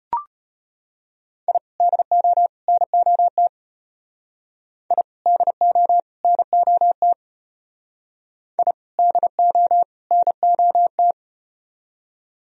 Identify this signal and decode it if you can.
Morse code, heard as a keyed tone, about 34 wpm, 700 Hz. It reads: I DO NOT I DO NOT I DO NOT